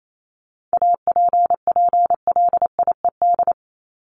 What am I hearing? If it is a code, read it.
Morse code, heard as a keyed tone, about 28 wpm, 700 Hz. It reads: APPLIED
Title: ED Suffix - Top 100 Words with Base Words 28wpm